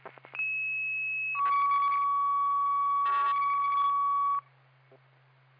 Two-tone page, heard as a one-second tone, then a three-second tone, about 2600 then 1130 Hz. It is heard from 0.4 to 4.4 s.